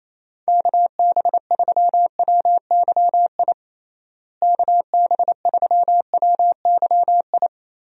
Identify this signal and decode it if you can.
Morse code, heard as a keyed tone, about 28 wpm, 700 Hz. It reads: KB3WYI KB3WYI
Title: USA Call Signs - 6 Characters - Repeated 3x - 28wpm